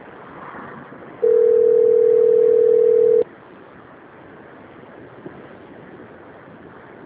Ringback tone (North America), 440 + 480 Hz.
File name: out- - - - - .11419